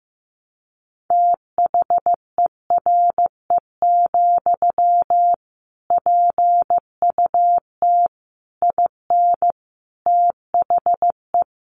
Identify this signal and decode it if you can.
Morse code, heard as a keyed tone, about 15 wpm, 700 Hz. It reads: THERE, PUT IN THE
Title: Sentences from Top 1200 Words 15wpm